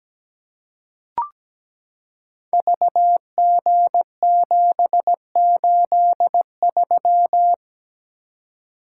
Morse code, heard as a keyed tone, about 17 wpm, 700 Hz. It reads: VG783
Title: Code Groups - 5 Letter:Numbers - 17wpm